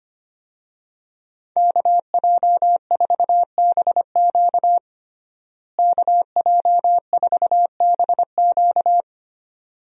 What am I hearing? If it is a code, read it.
Morse code, heard as a keyed tone, about 25 wpm, 700 Hz. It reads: KJ4BQ KJ4BQ
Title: USA Call Signs - 5 Characters - Repeated 2x - 25wpm